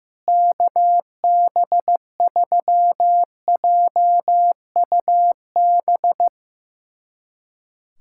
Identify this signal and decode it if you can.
Morse code, heard as a keyed tone, about 15 wpm, 700 Hz. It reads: KB3JUB
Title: USA Callsign DE Callsign - 6 Characters - 15wpm